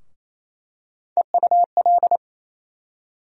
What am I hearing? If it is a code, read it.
Morse code, heard as a keyed tone, about 28 wpm, 700 Hz. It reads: EUL
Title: USA Call Sign Postfixes - 3 Letters 28wpm